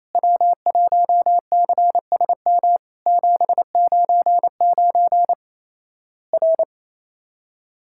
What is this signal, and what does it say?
Morse code, heard as a keyed tone, about 28 wpm, 700 Hz. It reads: W1CSM 799 R
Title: Worked All Europe DX Contest - QTC Exchange - 28wpm